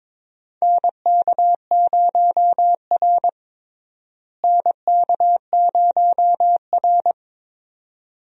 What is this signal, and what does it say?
Morse code, heard as a keyed tone, about 22 wpm, 700 Hz. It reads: NK0R NK0R
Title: USA Call Signs - 4 Characters - Repeated 2x - 22wpm